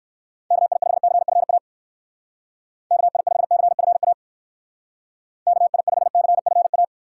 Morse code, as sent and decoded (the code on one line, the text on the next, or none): -.- .. ..... -.-. ..-. .- / -.- .. ..... -.-. ..-. .- / -.- .. ..... -.-. ..-. .-
KI5CFA KI5CFA KI5CFA